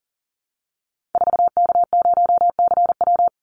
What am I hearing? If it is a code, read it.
Morse code, heard as a keyed tone, about 40 wpm, 700 Hz. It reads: 4K0CW
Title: Call Signs Repeated 3x - 40wpm